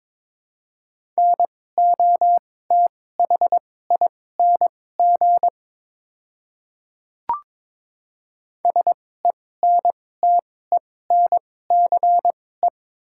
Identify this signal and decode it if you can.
Morse code, heard as a keyed tone, about 22 wpm, spaced as for 16 wpm, 700 Hz. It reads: NOTHING SENTENCE
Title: 2x Character Spacing - Top 500 Words - 22wpm